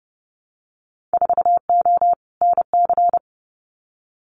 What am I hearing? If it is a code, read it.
Morse code, heard as a keyed tone, about 30 wpm, 700 Hz. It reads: 4O NC
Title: Winter Field Day Exchanges Encore 30wpm